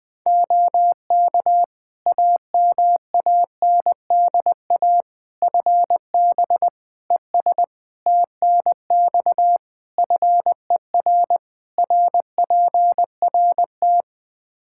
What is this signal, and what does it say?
Morse code, heard as a keyed tone, about 20 wpm, 700 Hz. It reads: OK AMANDA FB ES TNX FER RPRT
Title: K1USN SST QSO Practice 20wpm